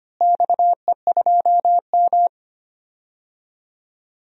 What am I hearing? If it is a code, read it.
Morse code, heard as a keyed tone, about 25 wpm, 700 Hz. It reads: XE2M
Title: DXCC-100 Call Signs - 4 Characters - Repeated 3x - 25wpm